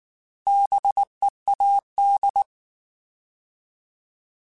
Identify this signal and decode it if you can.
Morse code, heard as a keyed tone, about 19 wpm, 800 Hz. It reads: BEAD
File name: Alphabetical / Alphabetical mp3s / Bs @18